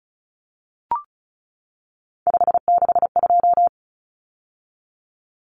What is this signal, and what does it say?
Morse code, heard as a keyed tone, about 34 wpm, 700 Hz. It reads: 562
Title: Common Cut Numbers - 3-Digit 35wpm